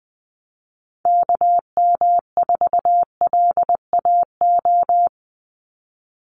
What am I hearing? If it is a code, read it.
Morse code, heard as a keyed tone, about 20 wpm, 700 Hz. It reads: KM4LAO
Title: Call Signs Repeated 3x - 20wpm